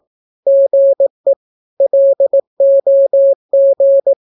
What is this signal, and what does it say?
Morse code, heard as a keyed tone, about 18 wpm, 550 Hz. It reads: GE LOG
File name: naukapodstaw / skroty / skroty